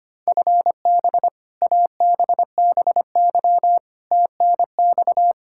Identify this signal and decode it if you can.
Morse code, heard as a keyed tone, about 25 wpm, 700 Hz. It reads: FB ABBY TNX
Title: Practice QSO - Exchange 3 - More Info - 25wpm